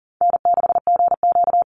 Morse code, heard as a keyed tone, about 37 wpm, 700 Hz. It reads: N6GQ
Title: Easy Call Signs - 40wpm